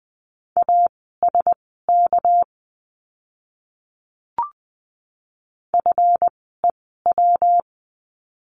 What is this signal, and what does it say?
Morse code, heard as a keyed tone, about 20 wpm, spaced as for 14 wpm, 700 Hz. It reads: ASK FEW
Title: 230 Lesson 37f - Letter x - Words 20wpm